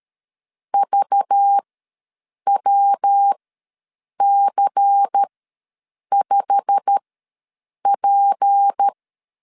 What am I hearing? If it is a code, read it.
Morse code, heard as a keyed tone, about 12 wpm, spaced as for 7 wpm, 800 Hz. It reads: VWC5P
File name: LEZIONI / 9 2867 ws 12wpm 3431